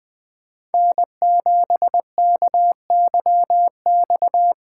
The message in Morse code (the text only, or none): N7KYX